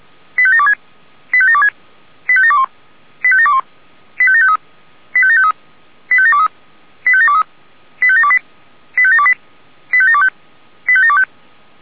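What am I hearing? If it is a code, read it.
Five-tone selective call (ZVEI), heard as quick runs of five tones, tones of about 70 ms. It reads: 85726 85726 85721 85721 85753 85753 85723 85723 85728 85728 85725 85725
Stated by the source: ZVEI five-tone selective call received on an SDR.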